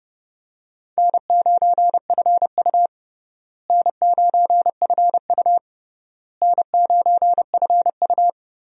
Morse code, heard as a keyed tone, about 30 wpm, 700 Hz. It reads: N9FU N9FU N9FU
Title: USA Call Signs - 4 Characters - Repeated 3x - 30wpm